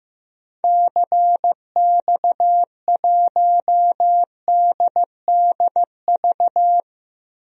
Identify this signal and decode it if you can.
Morse code, heard as a keyed tone, about 15 wpm, 700 Hz. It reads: CX1DDV
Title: DXCC-100 Call Signs - 6 Characters - 15wpm